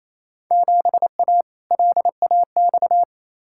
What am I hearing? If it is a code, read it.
Morse code, heard as a keyed tone, about 28 wpm, 700 Hz. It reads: 7A LAX
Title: Field Day Exchanges Encore - 28wpm